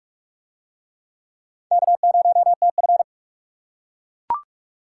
Morse code, heard as a keyed tone, about 44 wpm, 700 Hz. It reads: K0TF